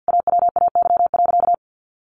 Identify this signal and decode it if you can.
Morse code, heard as a keyed tone, about 50 wpm, 700 Hz. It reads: AWAY.
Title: Sentences from Top 1500 Words 50wpm